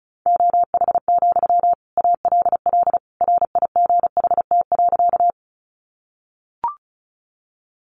Morse code, heard as a keyed tone, about 34 wpm, 700 Hz. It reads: OH, ALL RIGHT.
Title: Sentences from Top 500 Words 35wpm